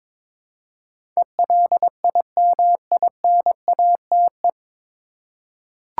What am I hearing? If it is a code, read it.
Morse code, heard as a keyed tone, about 21 wpm, 700 Hz. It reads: ELIMINATE
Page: Morse Code Ninja practice 22wpm